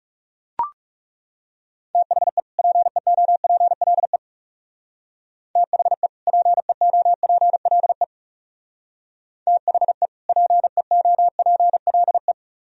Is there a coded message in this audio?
Yes: Morse code from 1.9 to 12.3 s, heard as a keyed tone, about 40 wpm, 700 Hz. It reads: THE PEOPLE THE PEOPLE THE PEOPLE